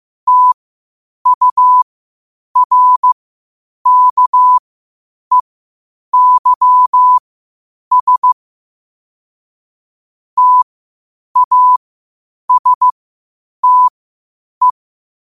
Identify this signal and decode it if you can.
Morse code, heard as a keyed tone, about 15 wpm, spaced as for 8 wpm, 1000 Hz. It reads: TURKEYS TASTE
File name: qotdfiles / 8wpm